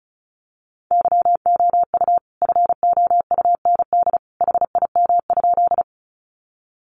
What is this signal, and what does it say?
Morse code, heard as a keyed tone, about 34 wpm, 700 Hz. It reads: YOU FOUND HIM?
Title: Sentences from Top 300 Words 35wpm